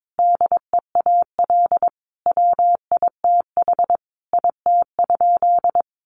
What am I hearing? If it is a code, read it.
Morse code, heard as a keyed tone, about 22 wpm, 700 Hz. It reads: DEAL WITH IT?
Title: Sentences from Top 1400 Words 22wpm